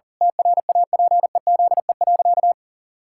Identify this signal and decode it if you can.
Morse code, heard as a keyed tone, about 40 wpm, 700 Hz. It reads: TRAPEZE.
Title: Pangram Challenge v2 - 40wpm